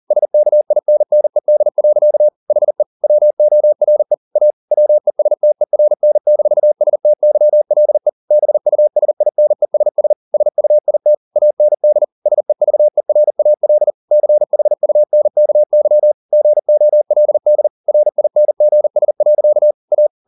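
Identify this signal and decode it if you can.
Morse code, heard as a keyed tone, about 40 wpm, 600 Hz. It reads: SKINNED. HE WORE A WESTERN-STYLE BUSINESS SUIT AND SEVERAL CHUNKY GOLD RINGS. A
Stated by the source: Morse code at 40 wpm, practice text, CW med Gnister.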